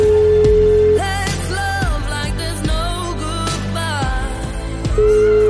Ringback tone (Europe, 425 Hz), with noise.